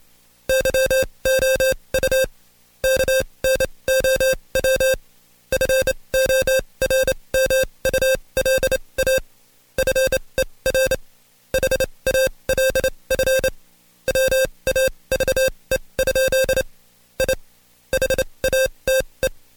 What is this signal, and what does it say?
Morse code, heard as a keyed tone, about 27 wpm, spaced as for 22 wpm, 530 Hz. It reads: YOU KNOW FORMULA FER HALF WAVE? I HATE
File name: Code Quick / AudioDownld2 / Seg137-Stepper10 21WPM